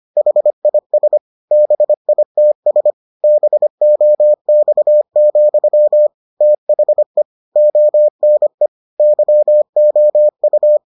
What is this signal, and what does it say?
Morse code, heard as a keyed tone, about 25 wpm, 600 Hz. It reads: HIS BITS BOX, THE ONE YOU